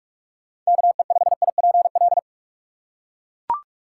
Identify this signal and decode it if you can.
Morse code, heard as a keyed tone, about 42 wpm, 700 Hz. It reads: KE5IPL